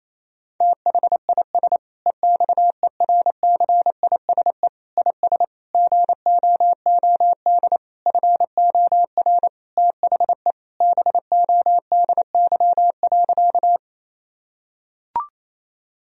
Morse code, heard as a keyed tone, about 28 wpm, 700 Hz. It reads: THIS EXERCISE IS GOOD FOR THE BODY.
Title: Sentences from Top 1000 Words - Review Part 4 - 28wpm